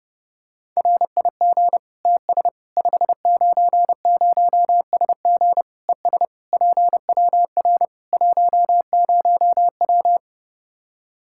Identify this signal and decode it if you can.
Morse code, heard as a keyed tone, about 31 wpm, 700 Hz. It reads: RIG TS 590SG ES PWR 10W
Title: Practice QSO - Exchange 2 - Info - 30wpm